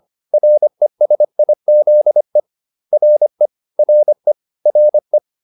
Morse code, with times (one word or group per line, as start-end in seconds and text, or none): 0.3-2.4 RESIZE
2.9-3.5 RE
3.8-4.3 RE
4.7-5.2 RE